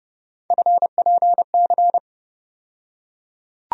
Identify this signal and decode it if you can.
Morse code, heard as a keyed tone, about 32 wpm, 700 Hz. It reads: FPC
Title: USA Call Sign Postfixes - 3 Letters 30wpm